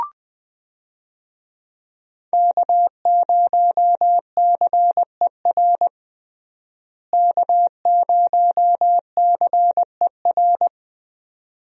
Morse code, heard as a keyed tone, about 20 wpm, 700 Hz. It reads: K0CER K0CER
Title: USA Call Signs - 5 Characters - Repeated 2x - 20wpm